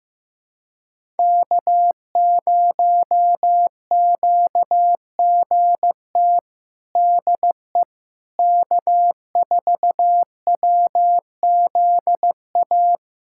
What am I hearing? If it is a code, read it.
Morse code, heard as a keyed tone, about 15 wpm, 700 Hz. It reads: K0QGT DE K4WZA
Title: USA Callsign DE Callsign - 5 Characters - 15wpm